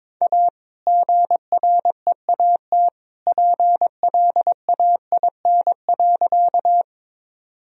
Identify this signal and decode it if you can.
Morse code, heard as a keyed tone, about 21 wpm, 700 Hz. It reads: A GREAT PLAIN.
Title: Sentences from Top 1000 Words - Review Part 2 - 22wpm